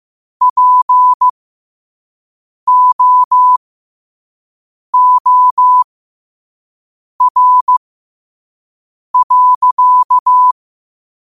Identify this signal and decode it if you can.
Morse code, heard as a keyed tone, about 15 wpm, spaced as for 5 wpm, 1000 Hz. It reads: POOR.
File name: qotdfiles / 5wpm